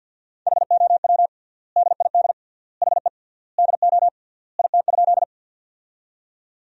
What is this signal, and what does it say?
Morse code, heard as a keyed tone, about 50 wpm, 700 Hz. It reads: HOW DID HE DO IT?
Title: 3x Word-Spacing - Sentences from Top 100 Words - 50wpm